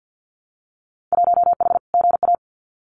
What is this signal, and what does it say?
Morse code, heard as a keyed tone, about 50 wpm, 700 Hz. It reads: 1H GA
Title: Winter Field Day Exchanges 50wpm